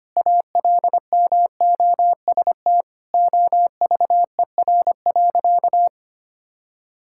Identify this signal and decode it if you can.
Morse code, heard as a keyed tone, about 25 wpm, 700 Hz. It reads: ALMOST OVER.
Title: Sentences from Top 1600 Words 25wpm